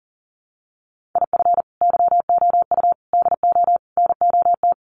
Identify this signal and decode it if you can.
Morse code, heard as a keyed tone, about 40 wpm, 700 Hz. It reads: IF YOU DO NOT
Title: Sets of 4 Words 40wpm